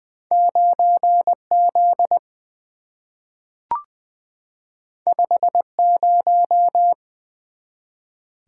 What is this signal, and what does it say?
Morse code, heard as a keyed tone, about 20 wpm, 700 Hz. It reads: 9Z 50
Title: Two Letter-Number Combinations - 20wpm